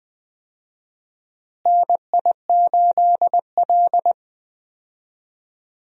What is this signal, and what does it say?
Morse code, heard as a keyed tone, about 20 wpm, 700 Hz. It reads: NI8L
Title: USA Call Signs - 4 Characters - Repeated 3x - 20wpm